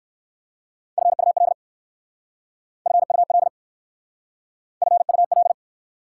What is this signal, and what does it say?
Morse code, heard as a keyed tone, about 60 wpm, 700 Hz. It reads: FUL FUL FUL